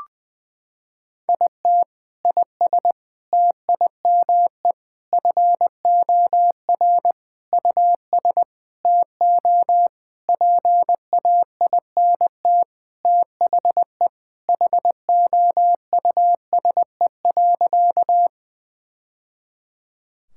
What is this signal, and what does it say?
Morse code, heard as a keyed tone, about 21 wpm, 700 Hz. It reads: IT IS TIME FOR US TO PAINT THE HOUSE.
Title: Sentences from Top 1000 Words - Review Part 6 - 20wpm